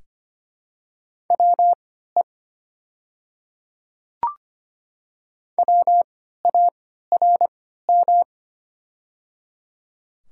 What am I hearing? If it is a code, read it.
Morse code, heard as a keyed tone, about 25 wpm, spaced as for 14 wpm, 700 Hz. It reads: WE WARM
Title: 3x Character Spacing - Top 500 Words - 25wpm